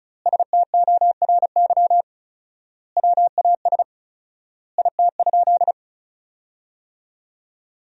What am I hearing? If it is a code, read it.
Morse code, heard as a keyed tone, about 36 wpm, 700 Hz. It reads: STORY WAS IT?
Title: 4x Word-Spacing - Sentences from Top 300 Words - 35wpm